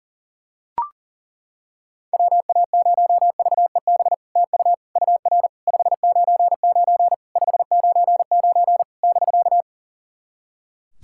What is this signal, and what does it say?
Morse code, heard as a keyed tone, about 37 wpm, 690 Hz. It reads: WA0VEB TU UR 599 599 <BK>
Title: SOTA QSO Practice 40wpm